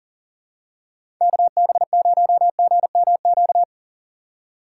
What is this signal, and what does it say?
Morse code, heard as a keyed tone, about 38 wpm, 700 Hz. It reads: KB0GMQ
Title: USA Call Signs - 4x Character Spacing - 40wpm